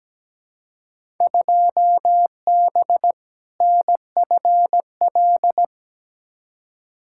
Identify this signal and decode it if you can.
Morse code, heard as a keyed tone, about 17 wpm, 700 Hz. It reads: 2B NFL
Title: Field Day Exchanges Encore - 17wpm